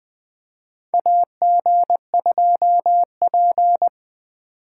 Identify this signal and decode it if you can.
Morse code, heard as a keyed tone, about 20 wpm, 700 Hz. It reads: AG2P
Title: USA Call Signs - 4 Characters - 4x Character Spacing - 20wpm